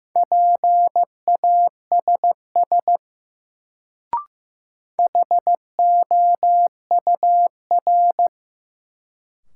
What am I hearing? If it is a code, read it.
Morse code, heard as a keyed tone, about 15 wpm, 700 Hz. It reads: PASS HOUR
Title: Sentences from Top 400 Words - New Word Drill - 15wpm